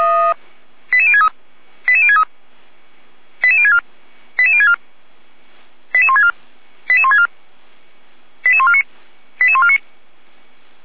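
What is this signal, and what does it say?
Five-tone selective call (ZVEI), heard as quick runs of five tones, tones of about 70 ms. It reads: 88063 88063 88064 88064 88164 88164 88149 88149
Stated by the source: ZVEI five-tone call 88102 received by SDR, from the Feuerwehr Orlamünde audio server.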